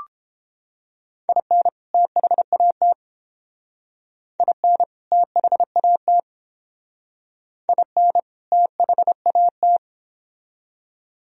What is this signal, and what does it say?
Morse code, heard as a keyed tone, about 28 wpm, 700 Hz. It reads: IN THAT IN THAT IN THAT